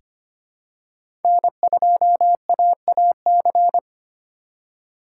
Morse code, heard as a keyed tone, about 25 wpm, 700 Hz. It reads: N2AAC
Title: USA Call Signs - 5 Characters - 3x Character Spacing - 25wpm